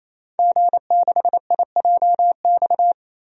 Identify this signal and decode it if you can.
Morse code, heard as a keyed tone, about 28 wpm, 700 Hz. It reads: G6IJX
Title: DXCC-100 Call Signs - 5 Characters - 28wpm